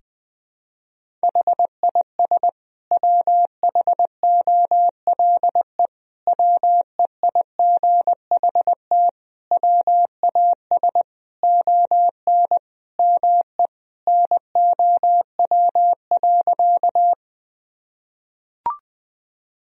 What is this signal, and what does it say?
Morse code, heard as a keyed tone, about 20 wpm, 700 Hz. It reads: HIS WHOLE WEIGHT WAS ON ME NOW.
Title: Sentences from Top 1000 Words - Review Part 3 - 20wpm